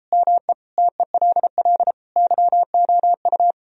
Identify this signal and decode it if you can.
Morse code, heard as a keyed tone, about 33 wpm, 700 Hz. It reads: ME TELL YOU